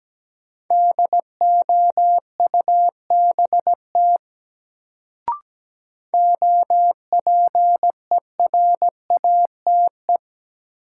Morse code, heard as a keyed tone, about 17 wpm, 700 Hz. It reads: DOUBT OPERATE
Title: Sentences from Top 1500 Words - New Word Drill Alt - 17wpm